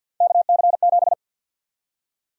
Morse code, heard as a keyed tone, about 50 wpm, 700 Hz. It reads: KC7
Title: USA Call Sign Prefixes - 50wpm